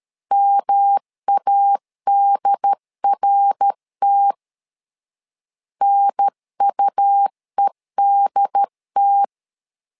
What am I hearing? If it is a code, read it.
Morse code, heard as a keyed tone, about 13 wpm, 800 Hz. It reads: MADRT NUEDT
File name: LEZIONI / 3 MWRO / 3 MWR 12wpm 1711